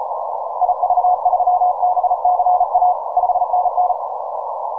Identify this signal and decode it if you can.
Morse code, heard as a keyed tone, about 35 wpm, 720 Hz, with noise. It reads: IV3BCA 5NN
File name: simple-file-list / VK0EK 10MCW